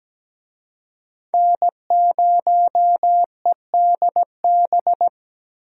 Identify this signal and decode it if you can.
Morse code, heard as a keyed tone, about 17 wpm, 700 Hz. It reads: N0EDB
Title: USA Call Signs - 5 Characters - 17wpm